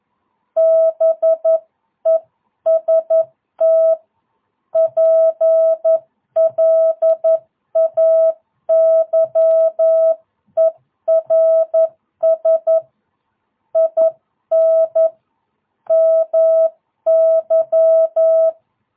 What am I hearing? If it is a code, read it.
Morse code, heard as a keyed tone, about 11 wpm, 660 Hz. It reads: BEST PLAYERS IN MY
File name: jh8cbh / 2022cw